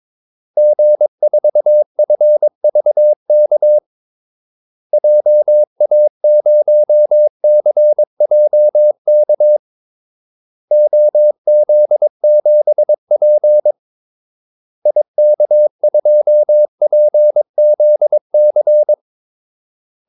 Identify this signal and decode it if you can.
Morse code, heard as a keyed tone, about 22 wpm, 600 Hz. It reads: G4FVK JA0CJK OZ7P IK2PZC
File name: Callsigns-22wpm